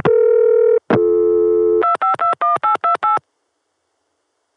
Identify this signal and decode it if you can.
DTMF, heard as telephone touch tones, tones of about 120 ms. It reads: 2521828